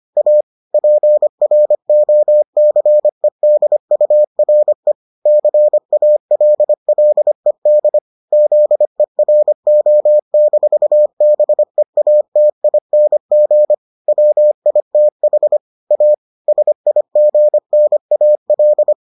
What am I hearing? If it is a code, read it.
Morse code, heard as a keyed tone, about 25 wpm, 600 Hz. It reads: A PROCEDURE CALLED ZERO-BEATING WITH A SIGNAL